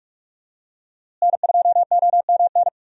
Morse code, heard as a keyed tone, about 46 wpm, 700 Hz. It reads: N2OMN
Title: USA Call Signs - 5 Characters - Repeated 3x - 45wpm